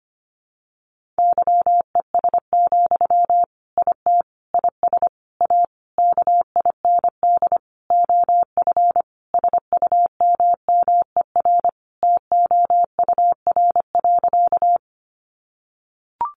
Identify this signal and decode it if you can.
Morse code, heard as a keyed tone, about 25 wpm, 700 Hz. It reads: YES, IT IS A KIND OF SUMMER TOUR.
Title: Sentences from Top 1400 Words 25wpm